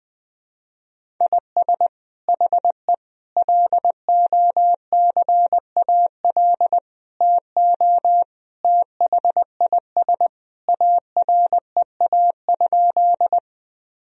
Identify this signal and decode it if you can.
Morse code, heard as a keyed tone, about 20 wpm, 700 Hz. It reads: IS HE LOCAL TO THIS AREA?
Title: Sentences from Top 1000 Words - Review Part 3 - 20wpm